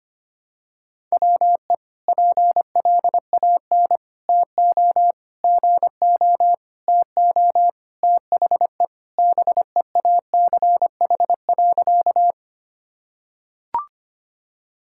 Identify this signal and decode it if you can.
Morse code, heard as a keyed tone, about 25 wpm, 700 Hz. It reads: WE PLAN TO GO TO THE BEACH.
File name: practice / 25wpm / Sentences with 4-Word Phrases - 25wpm